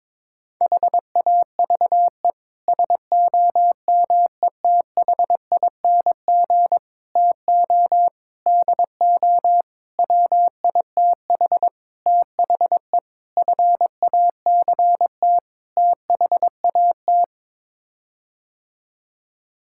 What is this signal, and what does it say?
Morse code, heard as a keyed tone, about 22 wpm, 700 Hz. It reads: HAVE SOMETHING TO DO WITH THE FACT THAT
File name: practice / 22wpm / Sets of 8 Words - 22wpm